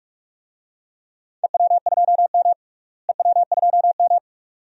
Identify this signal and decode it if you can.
Morse code, heard as a keyed tone, about 44 wpm, 700 Hz. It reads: EW2M EW2M
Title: DXCC-100 Call Signs - 4 Characters - Repeated 2x - 45wpm